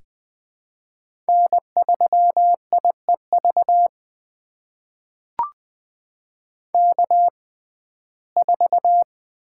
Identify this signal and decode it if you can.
Morse code, heard as a keyed tone, about 21 wpm, spaced as for 15 wpm, 700 Hz. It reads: N3IEV K4
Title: USA Call Signs - 5 Characters - 6x Character Spacing - 20wpm